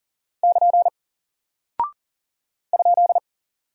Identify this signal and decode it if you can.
Morse code, heard as a keyed tone, about 40 wpm, 700 Hz. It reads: <KN> ?
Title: Single Character (Letter:Number:Punctuation:Prosign) 40wpm